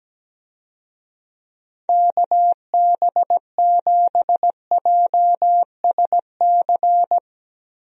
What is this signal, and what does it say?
Morse code, heard as a keyed tone, about 17 wpm, 700 Hz. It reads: KB7JSC
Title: USA Call Signs - 6 Characters - Repeated 2x - 17wpm